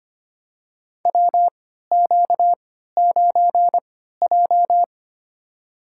Morse code, heard as a keyed tone, about 25 wpm, spaced as for 14 wpm, 700 Hz. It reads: WQ9J
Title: USA Call Signs - 4 Characters - 3x Character Spacing - 25wpm